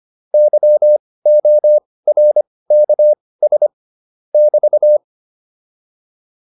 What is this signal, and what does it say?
Morse code, heard as a keyed tone, about 25 wpm, spaced as for 18 wpm, 600 Hz. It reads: YORKS <BT>